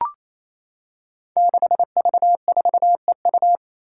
Morse code, heard as a keyed tone, about 29 wpm, 700 Hz. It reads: 6V4EU
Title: Five Challenging Letter-Number Combinations 28wpm